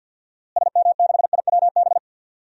Morse code, heard as a keyed tone, about 52 wpm, 700 Hz. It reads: SM6IWB
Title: DXCC-100 Call Signs - 6 Characters - Repeated 2x - 50wpm